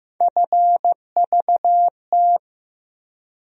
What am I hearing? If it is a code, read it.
Morse code, heard as a keyed tone, about 15 wpm, 700 Hz. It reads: FVT
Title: USA Call Sign Postfixes - 3 Letters 15wpm